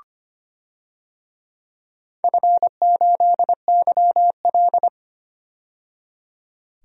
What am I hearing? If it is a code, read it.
Morse code, heard as a keyed tone, about 25 wpm, 700 Hz. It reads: F8YL
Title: DXCC-100 Call Signs - 4 Characters - 25wpm